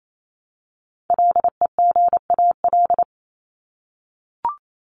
Morse code, heard as a keyed tone, about 28 wpm, 700 Hz. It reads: LEGAL
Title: Top 1001-1250 Words 28wpm